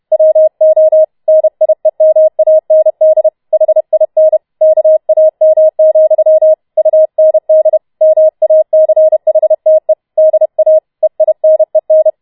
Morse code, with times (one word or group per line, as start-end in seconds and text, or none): 0.1-1.0 WO
1.3-3.3 NIEMAND
3.5-4.4 HIN
4.6-6.5 KAM,
6.8-7.8 UND
8.0-9.9 MACHTE
10.2-10.8 DA
11.0-12.1 EINEN